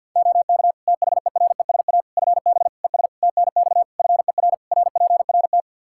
Morse code, heard as a keyed tone, about 50 wpm, 700 Hz. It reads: OK THERESA FB ES TNX FER RPRT
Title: Practice QSO - Exchange 2 - Info - 50wpm